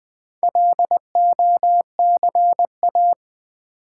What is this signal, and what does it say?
Morse code, heard as a keyed tone, about 20 wpm, 700 Hz. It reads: LOCA
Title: Build A Word - 2 Letters At A Time - 6-Letter Words - 20wpm